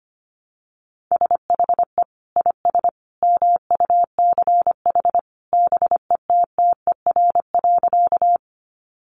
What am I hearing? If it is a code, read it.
Morse code, heard as a keyed tone, about 25 wpm, 700 Hz. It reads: SHE IS MUCH BETTER.
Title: Sentences from Top 400 Words 25wpm